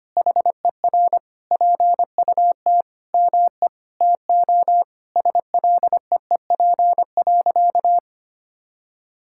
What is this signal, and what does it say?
Morse code, heard as a keyed tone, about 25 wpm, 700 Hz. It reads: HER PUT ME TO SLEEP.
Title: Sentences from Top 1600 Words 25wpm